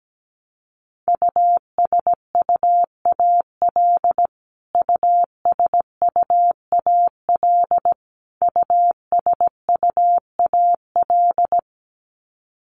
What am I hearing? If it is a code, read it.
Morse code, heard as a keyed tone, about 17 wpm, 700 Hz. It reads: USUAL USUAL USUAL
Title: Top 500 Words Repeated 17wpm